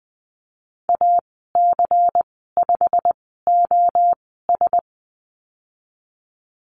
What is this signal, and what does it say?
Morse code, heard as a keyed tone, about 20 wpm, spaced as for 14 wpm, 700 Hz. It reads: AC5OS
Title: 154 Lesson 25g - Letter p - Call Signs 20wpm